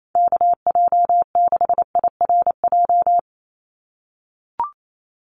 Morse code, heard as a keyed tone, about 28 wpm, 700 Hz. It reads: KJ6IRJ
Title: USA Call Signs 28wpm